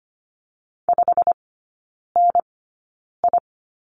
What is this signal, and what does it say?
Morse code, heard as a keyed tone, about 25 wpm, spaced as for 9 wpm, 700 Hz. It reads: 5NI
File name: course / 25:10wpm / 093 Lesson 16e - Letter c - Three Characters 25wpm